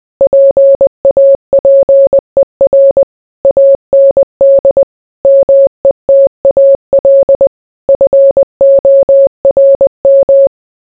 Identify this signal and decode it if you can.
Morse code, heard as a keyed tone, about 20 wpm, 550 Hz. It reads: PAPER AND METAL FORM